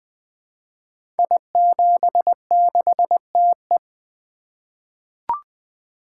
Morse code, heard as a keyed tone, about 20 wpm, 700 Hz. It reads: I76TE
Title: Code Groups - 5 Letter:Numbers - 20wpm